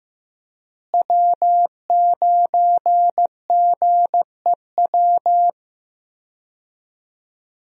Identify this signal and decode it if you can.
Morse code, heard as a keyed tone, about 15 wpm, 700 Hz. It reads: W9GEW
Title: USA Call Signs - 5 Characters - Repeated 2x - 15wpm